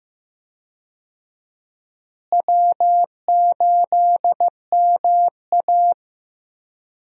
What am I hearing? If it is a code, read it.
Morse code, heard as a keyed tone, about 15 wpm, 700 Hz. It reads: W8MA
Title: USA Call Signs - 4 Characters - Repeated 2x - 15wpm